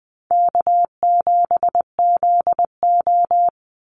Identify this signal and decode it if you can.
Morse code, heard as a keyed tone, about 20 wpm, 690 Hz. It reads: K7ZO
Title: NA SOTA Chaser Callsigns 20wpm